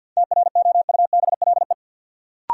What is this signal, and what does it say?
Morse code, heard as a keyed tone, about 47 wpm, 700 Hz. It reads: TROUBLE
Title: Sentences from Top 1200 Words - New Word Drill Alt - 50wpm